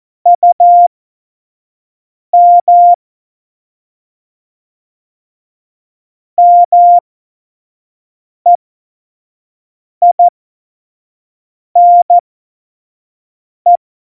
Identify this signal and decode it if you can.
Morse code, heard as a keyed tone, about 14 wpm, spaced as for 5 wpm, 700 Hz. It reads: UM MEINE